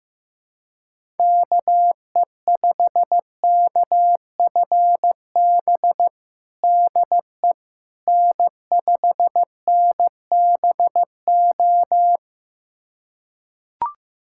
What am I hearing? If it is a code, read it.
Morse code, heard as a keyed tone, about 15 wpm, 700 Hz. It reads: KE5KFB DE N5NBO
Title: USA Callsign DE Callsign 15wpm